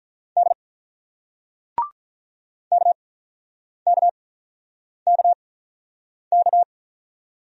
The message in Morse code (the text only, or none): D KKKK